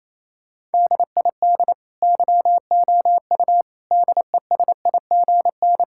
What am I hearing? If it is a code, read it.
Morse code, heard as a keyed tone, about 28 wpm, 700 Hz. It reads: DID YOU DESIGN
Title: Sentences from Top 1000 Words - Review Part 5 - 28wpm